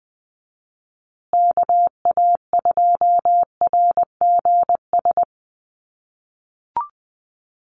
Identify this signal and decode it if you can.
Morse code, heard as a keyed tone, about 20 wpm, 700 Hz. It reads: KA2RGS
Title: USA Call Signs Repeated 3x 20wpm